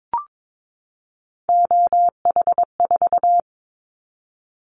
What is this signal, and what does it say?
Morse code, heard as a keyed tone, about 22 wpm, 700 Hz. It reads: OH4
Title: Callsign Prefixes for Top 25 DXCC - 22wpm